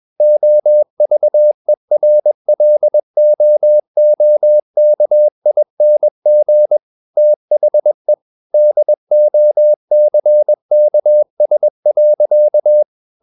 Morse code, heard as a keyed tone, about 21 wpm, 600 Hz. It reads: OVERLOOKING THE DOCKS.
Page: CW med Gnister